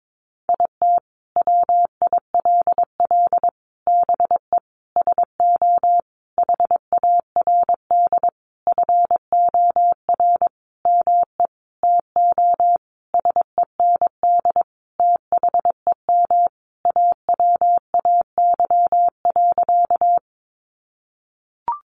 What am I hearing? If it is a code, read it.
Morse code, heard as a keyed tone, about 22 wpm, 700 Hz. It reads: IT WILL BE SO HARD FOR ME TO SEND THEM AWAY.
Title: Sentences from Top 1500 Words 22wpm